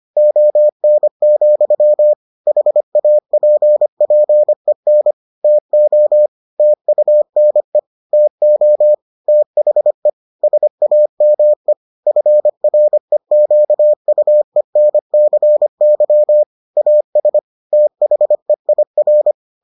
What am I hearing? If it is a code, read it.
Morse code, heard as a keyed tone, about 25 wpm, 600 Hz. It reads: ON, HAPPEN TO TUNE TO THE SAME FREQUENCY AS THEIR